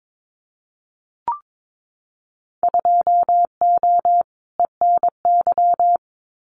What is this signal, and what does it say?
Morse code, heard as a keyed tone, about 22 wpm, 700 Hz. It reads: 2O ENY
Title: Winter Field Day Exchanges 22wpm